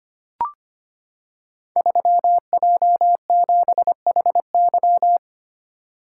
Morse code, heard as a keyed tone, about 25 wpm, 700 Hz. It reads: 3J7HY
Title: Code Groups - 5 Letter:Numbers - 25wpm